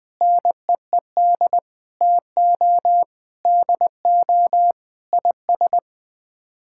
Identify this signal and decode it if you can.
Morse code, heard as a keyed tone, about 21 wpm, 700 Hz. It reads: NEED TO DO IS
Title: Sets of 6 Words - 20wpm